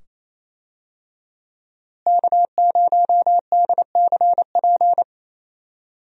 Morse code, heard as a keyed tone, about 28 wpm, 700 Hz. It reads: K0DCP
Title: USA Call Signs - 5 Characters - Repeated 2x - 28wpm